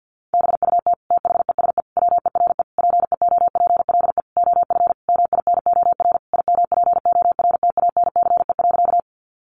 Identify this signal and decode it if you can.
Morse code, heard as a keyed tone, about 50 wpm, 700 Hz. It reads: BUT THESE WERE PEOPLE OF MINOR IMPORTANCE.
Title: Sentences from Top 1500 Words 50wpm